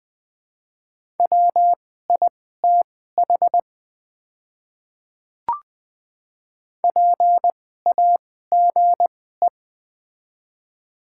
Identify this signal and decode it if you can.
Morse code, heard as a keyed tone, about 20 wpm, spaced as for 14 wpm, 700 Hz. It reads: WITH PAGE
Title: 2x Character Spacing - Top 300 Words - 20wpm